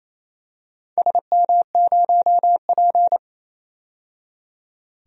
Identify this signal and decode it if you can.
Morse code, heard as a keyed tone, about 28 wpm, 700 Hz. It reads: SM0P
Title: DXCC-100 Call Signs - 4 Characters - 28wpm